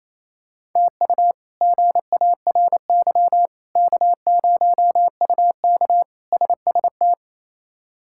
Morse code, heard as a keyed tone, about 28 wpm, 700 Hz. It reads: TU GARY K0UK SST
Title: K1USN SST QSO Practice - 28wpm